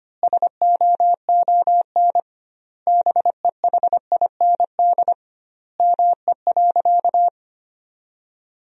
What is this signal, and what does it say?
Morse code, heard as a keyed tone, about 25 wpm, 700 Hz. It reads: SOON BEHIND ME.
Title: 2x Word-Spacing - Sentences from Top 500 Words - 25wpm